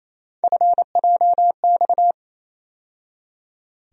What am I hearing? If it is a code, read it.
Morse code, heard as a keyed tone, about 28 wpm, 700 Hz. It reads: FJX
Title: USA Call Sign Postfixes - 3 Letters 28wpm